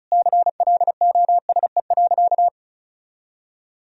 Morse code, heard as a keyed tone, about 36 wpm, 700 Hz. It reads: CLOSE.
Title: Sentences from Top 1000 Words - Review Part 1 - 35wpm